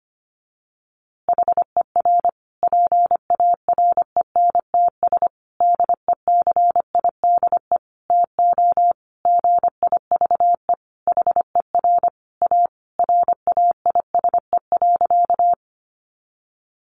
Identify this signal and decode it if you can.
Morse code, heard as a keyed tone, about 25 wpm, 700 Hz. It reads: HER PARENTS DECIDE TO GIVE HER A RAISE.
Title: Easy Phrases 25wpm